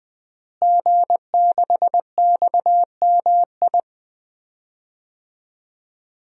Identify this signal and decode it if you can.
Morse code, heard as a keyed tone, about 20 wpm, 700 Hz. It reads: G6XMI
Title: DXCC-100 Call Signs - 5 Characters - Repeated 2x - 20wpm